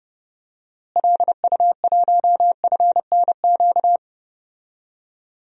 Morse code, heard as a keyed tone, about 30 wpm, 700 Hz. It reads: LU1FNQ
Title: DXCC-100 Call Signs - 6 Characters - Repeated 2x - 30wpm